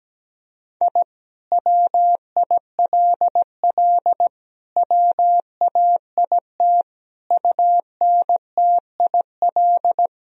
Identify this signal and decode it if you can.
Morse code, heard as a keyed tone, about 17 wpm, 700 Hz. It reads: I WILL WAIT UNTIL
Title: Sentences from Top 1000 Words - Review Part 1 - 17wpm